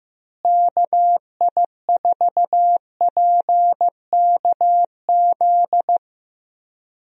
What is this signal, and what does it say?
Morse code, heard as a keyed tone, about 15 wpm, 700 Hz. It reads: KI4PKZ
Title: USA Call Signs - 6 Characters - Repeated 3x - 15wpm